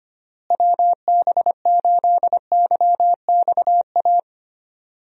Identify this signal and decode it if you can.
Morse code, heard as a keyed tone, about 25 wpm, 700 Hz. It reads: WB8YXA
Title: USA Call Signs - 6 Characters - Repeated 2x - 25wpm